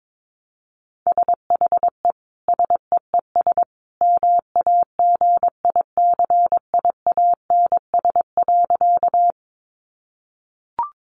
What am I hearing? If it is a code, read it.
Morse code, heard as a keyed tone, about 22 wpm, 700 Hz. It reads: SHE SEES MAGICIANS.
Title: Easy Phrases 22wpm